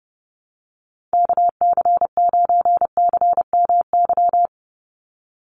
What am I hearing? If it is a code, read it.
Morse code, heard as a keyed tone, about 30 wpm, 700 Hz. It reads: KC9CMY
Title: USA Call Signs 30wpm